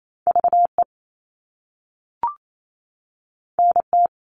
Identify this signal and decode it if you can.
Morse code, heard as a keyed tone, about 28 wpm, 700 Hz. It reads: VE NT
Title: All Cut Numbers - 2-Digit 28wpm